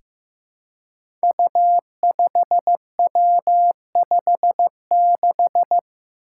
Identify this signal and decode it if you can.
Morse code, heard as a keyed tone, about 15 wpm, 700 Hz. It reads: U5W56
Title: Five Challenging Letter-Number Combinations 15wpm